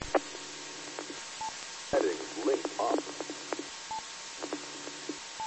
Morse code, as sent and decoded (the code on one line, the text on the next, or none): none